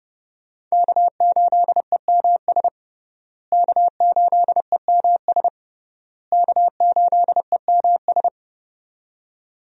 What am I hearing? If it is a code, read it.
Morse code, heard as a keyed tone, about 30 wpm, 700 Hz. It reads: K8EMS K8EMS K8EMS
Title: USA Call Signs - 5 Characters - Repeated 3x - 30wpm